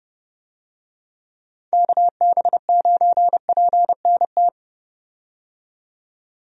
Morse code, heard as a keyed tone, about 30 wpm, 700 Hz. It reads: KB9PNT
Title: USA Call Signs - 6 Characters - 30wpm